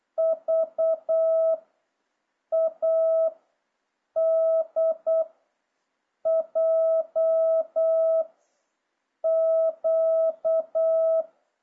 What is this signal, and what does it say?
Morse code, heard as a keyed tone, about 8 wpm, 650 Hz. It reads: V A D J Q